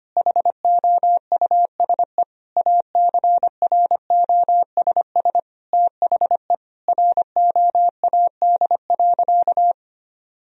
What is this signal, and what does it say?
Morse code, heard as a keyed tone, about 25 wpm, 700 Hz. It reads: HOUSE ACROSS THE ROAD.
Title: Sentences from Top 1000 Words - Review Part 1 - 25wpm